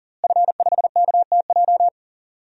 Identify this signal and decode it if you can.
Morse code, heard as a keyed tone, about 40 wpm, 700 Hz. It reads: F5KTJ
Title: DXCC-100 Call Signs - 5 Characters - Repeated 3x - 40wpm